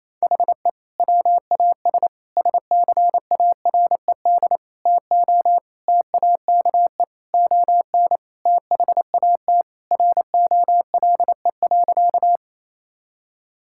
Morse code, heard as a keyed tone, about 28 wpm, 700 Hz. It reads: HE WAS SCARED TO TAKE ON THAT ROLE.
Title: Sentences from Top 1000 Words - Review Part 6 - 28wpm